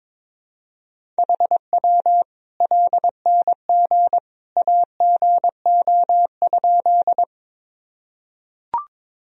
Morse code, heard as a keyed tone, about 22 wpm, 700 Hz. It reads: HW LNG AGO?
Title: Sentences from Top 500 Words and Abbreviations 22wpm